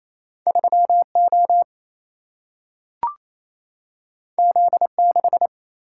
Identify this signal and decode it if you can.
Morse code, heard as a keyed tone, about 28 wpm, 700 Hz. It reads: 3 O Z 6
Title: Single-Letter Single-Number 28wpm